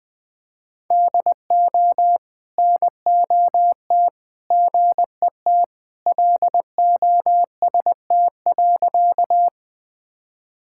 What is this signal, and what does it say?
Morse code, heard as a keyed tone, about 20 wpm, 700 Hz. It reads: DO NOT GET LOST.
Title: Sentences with 3-Word Phrases - 20wpm